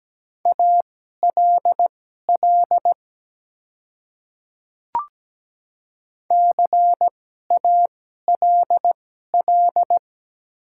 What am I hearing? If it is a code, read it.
Morse code, heard as a keyed tone, about 17 wpm, spaced as for 12 wpm, 700 Hz. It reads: ALL CALL
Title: 2x Character Spacing - Top 100 Words - 17wpm